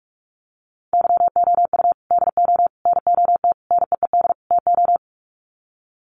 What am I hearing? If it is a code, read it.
Morse code, heard as a keyed tone, about 44 wpm, 700 Hz. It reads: YOU DO NOT NEED TO
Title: Sets of 5 Words 45wpm